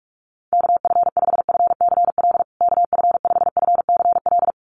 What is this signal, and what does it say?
Morse code, heard as a keyed tone, about 44 wpm, 700 Hz. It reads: KF5FCL KF5FCL
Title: USA Call Signs Repeated 2x 45wpm